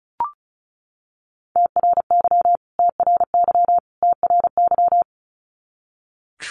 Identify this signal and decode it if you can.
Morse code, heard as a keyed tone, about 34 wpm, 700 Hz. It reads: TRY TRY TRY
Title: Top 300 Words Repeated 35wpm